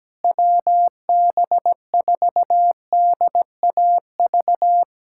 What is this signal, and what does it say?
Morse code, heard as a keyed tone, about 17 wpm, 700 Hz. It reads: WB4DAV
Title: USA Callsign DE Callsign 17wpm